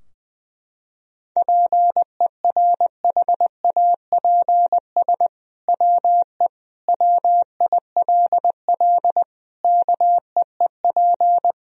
Morse code, heard as a keyed tone, about 21 wpm, 700 Hz. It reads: PERHAPS WE WILL KEEP
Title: Sentences from Top 1000 Words - Review Part 3 - 20wpm